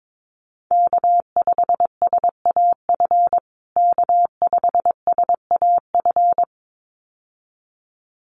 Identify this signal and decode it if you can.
Morse code, heard as a keyed tone, about 22 wpm, 700 Hz. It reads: K5SAF K5SAF
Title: USA Call Signs Repeated 2x 22wpm